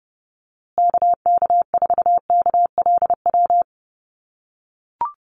Morse code, heard as a keyed tone, about 30 wpm, 700 Hz. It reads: KK4KLW
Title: USA Call Signs Repeated 3x 30wpm